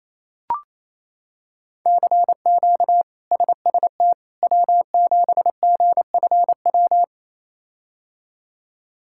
Morse code, heard as a keyed tone, about 28 wpm, 700 Hz. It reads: CQ SST W7GFW
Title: K1USN SST QSO Practice - 28wpm